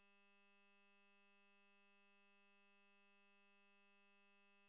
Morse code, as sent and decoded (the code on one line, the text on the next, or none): none